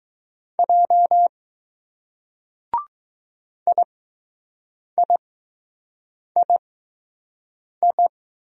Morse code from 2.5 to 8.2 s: IIII

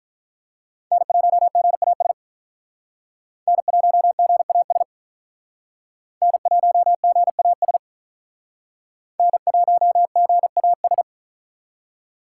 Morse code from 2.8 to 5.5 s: N1GAS

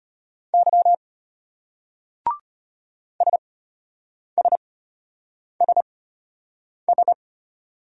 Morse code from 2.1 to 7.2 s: SSSS

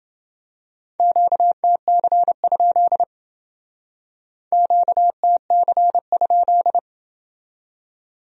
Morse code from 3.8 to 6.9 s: QTC?

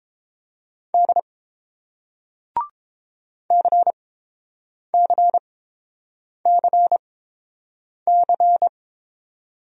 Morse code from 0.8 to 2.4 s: D

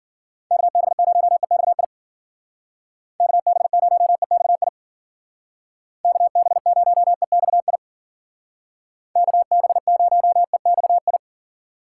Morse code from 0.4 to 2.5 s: KB0EXI

